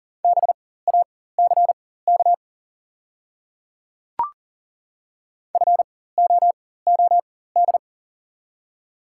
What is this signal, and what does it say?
Morse code, heard as a keyed tone, about 40 wpm, spaced as for 19 wpm, 690 Hz. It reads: BACK FOOD
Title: 4x Character Spacing - Top 300 Words - 40wpm